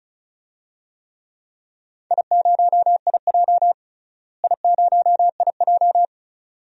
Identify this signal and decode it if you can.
Morse code, heard as a keyed tone, about 34 wpm, 700 Hz. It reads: I0IJ I0IJ
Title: DXCC-100 Call Signs - 4 Characters - Repeated 2x - 35wpm